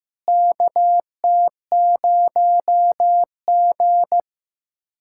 Morse code, heard as a keyed tone, about 16 wpm, 700 Hz. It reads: KT0G
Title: K1USN SST QSO Practice - 15wpm